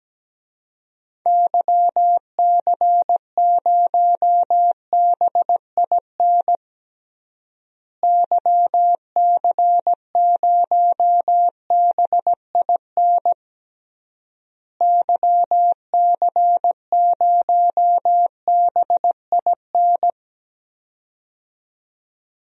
Morse code, heard as a keyed tone, about 17 wpm, 700 Hz. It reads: YC0BIN YC0BIN YC0BIN
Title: DXCC-100 Call Signs - 6 Characters - Repeated 3x - 17wpm